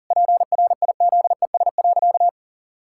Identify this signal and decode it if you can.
Morse code, heard as a keyed tone, about 40 wpm, 700 Hz. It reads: PRIZES.
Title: Pangram Challenge v2 - 40wpm